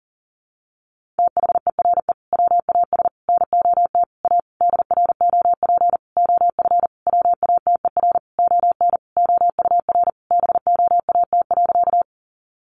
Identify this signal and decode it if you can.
Morse code, heard as a keyed tone, about 40 wpm, 700 Hz. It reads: THERE WAS NOT A DROP OF WATER ON OUR BOAT.
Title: Sentences from Top 1400 Words 40wpm